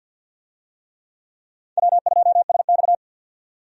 Morse code, heard as a keyed tone, about 50 wpm, 700 Hz. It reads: W2SX
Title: USA Call Signs - 4 Characters - 50wpm